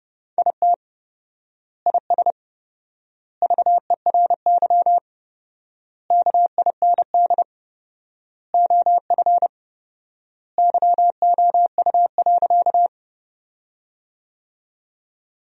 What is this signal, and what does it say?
Morse code, heard as a keyed tone, about 32 wpm, 700 Hz. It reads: IT IS VERY KIND OF YOU.
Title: 4x Word-Spacing - Sentences from Top 200 Words - 30wpm